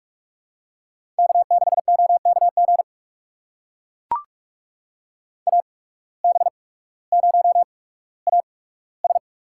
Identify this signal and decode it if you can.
Morse code, heard as a keyed tone, about 44 wpm, spaced as for 19 wpm, 700 Hz. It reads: K6OKG AB0AS
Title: USA Call Signs - 5 Characters - 8x Character Spacing - 45wpm